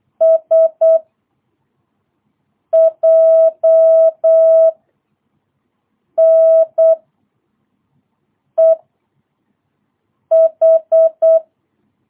Morse code, heard as a keyed tone, about 8 wpm, 650 Hz. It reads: S J N E H
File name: jh8cbh / 2022cw